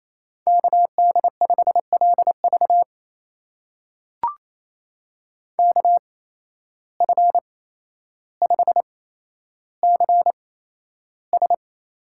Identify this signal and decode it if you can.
Morse code, heard as a keyed tone, about 27 wpm, spaced as for 12 wpm, 700 Hz. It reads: KD5LV KF5CS
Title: USA Call Signs - 5 Characters - 8x Character Spacing - 28wpm